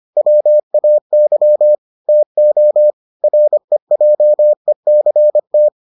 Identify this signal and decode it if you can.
Morse code, heard as a keyed tone, about 25 wpm, 600 Hz. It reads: WAY TO REJECT